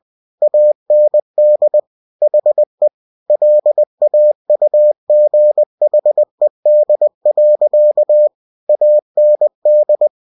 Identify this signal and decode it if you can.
Morse code, heard as a keyed tone, about 20 wpm, 600 Hz. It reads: AND HE LAUGHED. AND
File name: cw-books / winnie-the-pooh / winnie-20wpm-page-0043